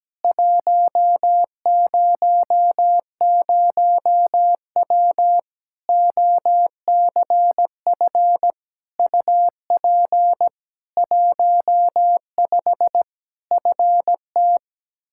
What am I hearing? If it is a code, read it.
Morse code, heard as a keyed tone, about 17 wpm, 700 Hz. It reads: 100W OCF UP 15 FT
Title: Practice QSO - Exchange 2 - Info - 17wpm